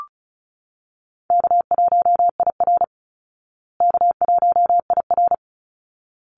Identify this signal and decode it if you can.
Morse code, heard as a keyed tone, about 34 wpm, 700 Hz. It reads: K1IR K1IR
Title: Easy Call Signs Repeated 2x - 35wpm